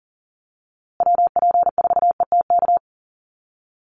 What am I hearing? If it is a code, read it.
Morse code, heard as a keyed tone, about 40 wpm, 700 Hz. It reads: WP4ETK
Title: USA Call Signs Repeated 2x 40wpm